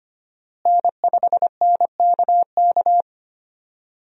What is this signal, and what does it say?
Morse code, heard as a keyed tone, about 25 wpm, 700 Hz. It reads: N5NKK
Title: USA Call Signs - 5 Characters - Repeated 2x - 25wpm